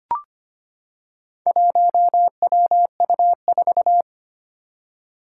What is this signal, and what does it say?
Morse code, heard as a keyed tone, about 25 wpm, 700 Hz. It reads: 1WU4
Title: Four Challenging Letter-Number Combinations 25wpm